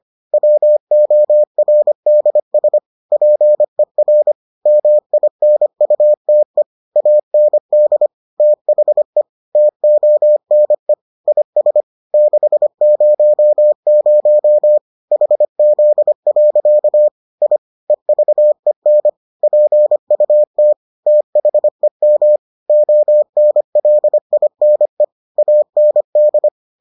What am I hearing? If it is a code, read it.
Morse code, heard as a keyed tone, about 25 wpm, 600 Hz. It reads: WORDS PER MINUTE AND THE TONE IS 600 HZ. I EVEN PUT THEM ONLINE AND